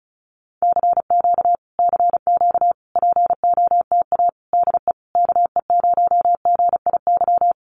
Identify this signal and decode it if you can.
Morse code, heard as a keyed tone, about 34 wpm, 700 Hz. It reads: CQ CQ POTA DE KE0GIY
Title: Common QSO Phrases Encore 35wpm